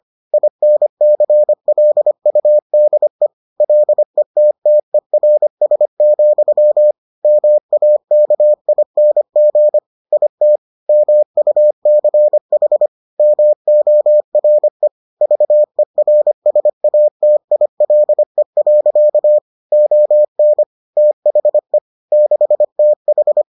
Morse code, heard as a keyed tone, about 25 wpm, 600 Hz. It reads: INCLUDE LETTERS, MAKING IT MUCH MORE VERSATILE. ON THE 6TH